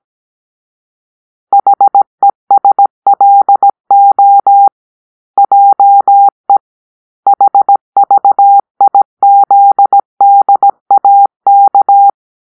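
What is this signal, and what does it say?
Morse code, heard as a keyed tone, about 17 wpm, 800 Hz. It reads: HESLO JE HVIZDAK
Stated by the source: Morse-code cipher made by a player team for the Kachna puzzle game.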